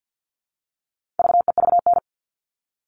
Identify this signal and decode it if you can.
Morse code, heard as a keyed tone, about 50 wpm, 700 Hz. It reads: VEVN
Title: All Cut Numbers - 4-Digit 50wpm